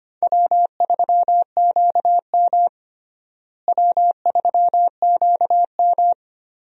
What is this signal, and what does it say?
Morse code, heard as a keyed tone, about 25 wpm, 700 Hz. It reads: W3QM W3QM
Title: USA Call Signs - 4 Characters - Repeated 3x - 25wpm